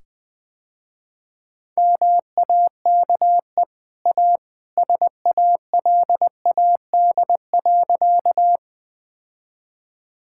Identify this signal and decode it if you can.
Morse code, heard as a keyed tone, about 21 wpm, 700 Hz. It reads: MAKE A SALAD.
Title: Easy Sentences with Hints - 20wpm